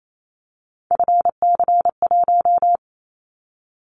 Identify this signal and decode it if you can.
Morse code, heard as a keyed tone, about 28 wpm, 700 Hz. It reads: FC1
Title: Callsign Prefixes for Top 25 DXCC - Repeated 3x - 28wpm